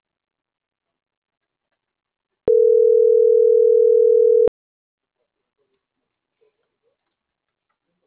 Ringback tone (North America), 440 + 480 Hz.